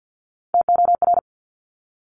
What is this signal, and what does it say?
Morse code, heard as a keyed tone, about 50 wpm, 700 Hz. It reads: TOR
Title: Top Three Letter Combinations in Words 50wpm